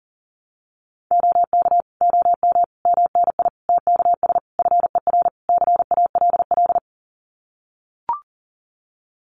Morse code, heard as a keyed tone, about 40 wpm, 700 Hz. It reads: OK OM MNI TKS FER CALL
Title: Common QSO Phrases 40wpm